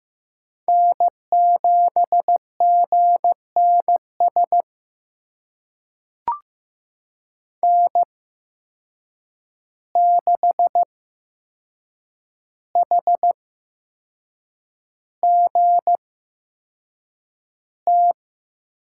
Morse code, heard as a keyed tone, about 15 wpm, spaced as for 6 wpm, 700 Hz. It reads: N7GNS N6HGT
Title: USA Call Signs - 5 Characters - 8x Character Spacing - 15wpm